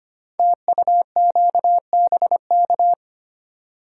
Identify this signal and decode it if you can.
Morse code, heard as a keyed tone, about 25 wpm, 700 Hz. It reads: TUQBK
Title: Code Groups - 5 Letters - 25wpm